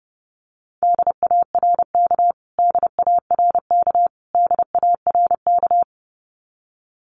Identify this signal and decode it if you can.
Morse code, heard as a keyed tone, about 28 wpm, 700 Hz. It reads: DARK DARK DARK
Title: Top 500 Words Repeated 30wpm